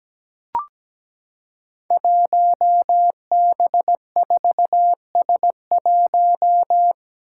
Morse code, heard as a keyed tone, about 17 wpm, 700 Hz. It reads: 1B4S1
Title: Five Challenging Letter-Number Combinations 17wpm